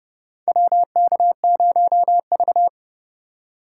Morse code, heard as a keyed tone, about 30 wpm, 700 Hz. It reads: WK0V
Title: USA Call Signs - 4 Characters - 4x Character Spacing - 30wpm